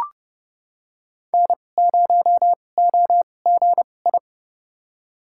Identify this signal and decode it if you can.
Morse code, heard as a keyed tone, about 32 wpm, spaced as for 22 wpm, 700 Hz. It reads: N0OGI
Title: USA Call Signs - 5 Characters - 2x Character Spacing - 30wpm